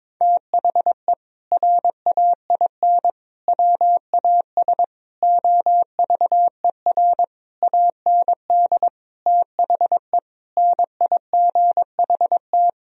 Morse code, heard as a keyed tone, about 22 wpm, 700 Hz. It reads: THE RAIN WAS OVER AND THE NIGHT
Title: Sentences from Top 1000 Words - Review Part 1 - 22wpm